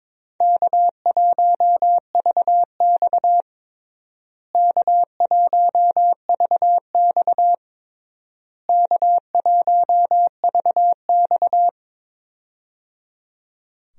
Morse code, heard as a keyed tone, about 22 wpm, 700 Hz. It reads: K1VX K1VX K1VX
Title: USA Call Signs - 4 Characters - Repeated 3x - 22wpm